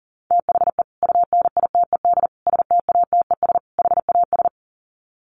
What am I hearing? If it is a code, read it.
Morse code, heard as a keyed tone, about 40 wpm, 700 Hz. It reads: THE UNITED STATES HAS
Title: Sets of 4 Words 40wpm